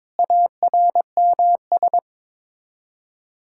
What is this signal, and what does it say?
Morse code, heard as a keyed tone, about 22 wpm, 700 Hz. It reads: ARMS
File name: practice / 22wpm / Sets of 7 Words - 22wpm